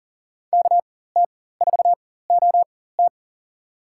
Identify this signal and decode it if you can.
Morse code, heard as a keyed tone, about 40 wpm, spaced as for 19 wpm, 700 Hz. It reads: KT4OT
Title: USA Call Signs - 5 Characters - 4x Character Spacing - 40wpm